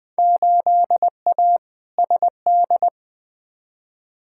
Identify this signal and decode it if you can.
Morse code, heard as a keyed tone, about 20 wpm, 700 Hz. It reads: 8A SD
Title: Field Day Exchanges Encore - 20wpm